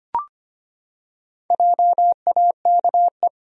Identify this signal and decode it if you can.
Morse code, heard as a keyed tone, about 25 wpm, 700 Hz. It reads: JAKE
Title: USA Common Names - 25wpm